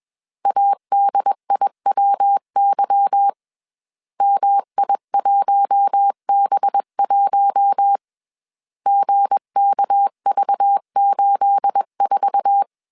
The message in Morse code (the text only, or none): ADIWY MI1B1 GKV84